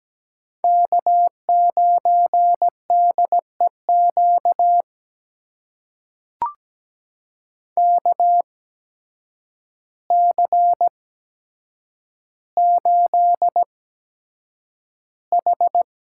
Morse code, heard as a keyed tone, about 17 wpm, spaced as for 8 wpm, 700 Hz. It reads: K9DEQ KC8H
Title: USA Call Signs - 8x Character Spacing - 17wpm